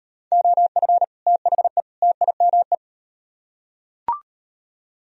Morse code, heard as a keyed tone, about 38 wpm, 700 Hz. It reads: OF THE TIME